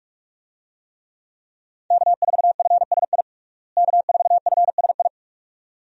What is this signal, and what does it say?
Morse code, heard as a keyed tone, about 46 wpm, 700 Hz. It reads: K4FSI K4FSI
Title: USA Call Signs - 5 Characters - Repeated 2x - 45wpm